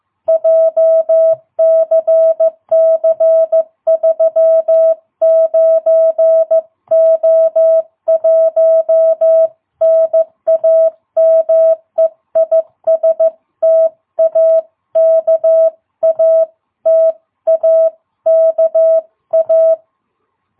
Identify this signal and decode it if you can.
Morse code, heard as a keyed tone, about 14 wpm, 660 Hz. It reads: JCC39O1 NAME IS TAKA TAKA
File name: jh8cbh / 2022cw